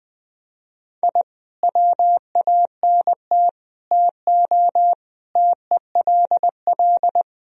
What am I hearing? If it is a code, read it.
Morse code, heard as a keyed tone, about 21 wpm, 700 Hz. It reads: I WANT TO TELL
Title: Sentences from Top 1000 Words - Review Part 4 - 20wpm